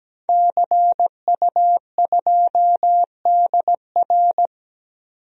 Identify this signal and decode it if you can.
Morse code, heard as a keyed tone, about 17 wpm, 700 Hz. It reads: CU2DR
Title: DXCC-100 Call Signs - 5 Characters - 17wpm